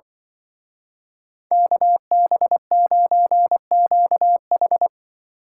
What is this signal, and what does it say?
Morse code, heard as a keyed tone, about 23 wpm, 700 Hz. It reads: KB9QH
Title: Speed Racing - USA 2x2 Call Signs 20wpm